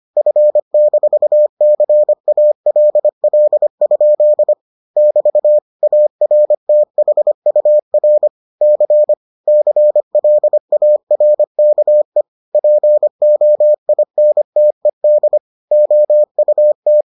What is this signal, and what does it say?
Morse code, heard as a keyed tone, about 25 wpm, 600 Hz. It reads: F-CALL? <BT> ARTHUR C CLARKE POINTED OUT